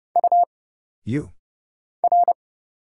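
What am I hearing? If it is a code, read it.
Morse code, heard as a keyed tone, about 30 wpm, 700 Hz. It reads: U R
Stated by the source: Morse Code Ninja single-character practice (letters, numbers, punctuation, prosigns) at 30 wpm, by Kurt Zoglmann (AD0WE).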